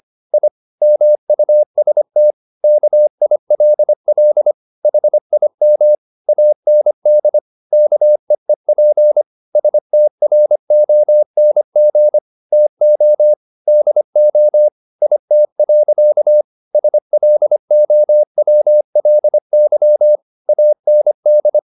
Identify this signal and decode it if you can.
Morse code, heard as a keyed tone, about 25 wpm, 600 Hz. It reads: I MUST KILL HIM AND KEEP STRONG TO DO IT. SLOWLY AND
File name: the old man and the sea part 1 25wpm